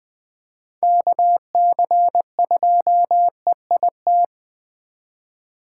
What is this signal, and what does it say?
Morse code, heard as a keyed tone, about 20 wpm, 700 Hz. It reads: KC2EIT
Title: Speed Racing - USA 2x3 Call Signs 17wpm